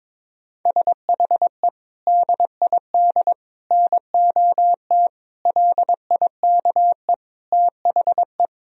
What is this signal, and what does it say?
Morse code, heard as a keyed tone, about 22 wpm, 700 Hz. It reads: SHE DID NOT LIKE THE
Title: Sentences from Top 1000 Words - Review Part 3 - 22wpm